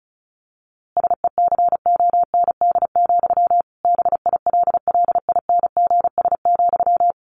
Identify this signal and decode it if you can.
Morse code, heard as a keyed tone, about 34 wpm, 700 Hz. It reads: SECOND, BILLINGS,
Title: A Sound Of Thunder 35wpm